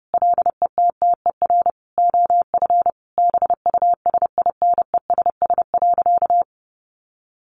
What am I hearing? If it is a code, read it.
Morse code, heard as a keyed tone, about 30 wpm, 700 Hz. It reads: LETTER OF BUSINESS.
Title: Sentences from Top 1400 Words 30wpm